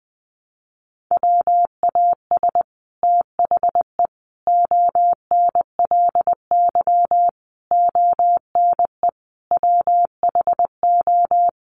Morse code, heard as a keyed tone, about 20 wpm, 700 Hz. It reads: WAS THE ONLY ONE WHO
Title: Sets of 5 Words Encore2 20wpm